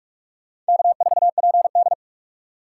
Morse code, heard as a keyed tone, about 48 wpm, 700 Hz. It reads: K4PD
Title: USA Call Signs - 6x Character Spacing - 45wpm